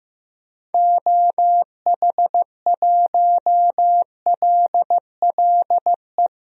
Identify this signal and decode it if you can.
Morse code, heard as a keyed tone, about 15 wpm, 700 Hz. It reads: OH1LLE
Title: DXCC-100 Call Signs - 6 Characters - Repeated 2x - 15wpm